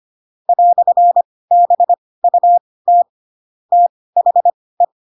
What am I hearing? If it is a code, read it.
Morse code, heard as a keyed tone, about 25 wpm, spaced as for 18 wpm, 700 Hz. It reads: "BUT THE
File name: Pinocchio / Pinocchio ch7 12 25 ik1hge